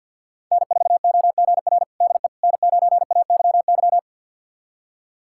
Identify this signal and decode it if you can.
Morse code, heard as a keyed tone, about 51 wpm, 700 Hz. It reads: N4OGR DE N9AYY
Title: USA Callsign DE Callsign - 5 Characters - 50wpm